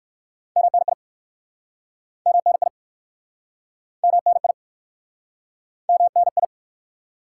Morse code, heard as a keyed tone, about 60 wpm, 700 Hz. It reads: MNI MNI MNI MNI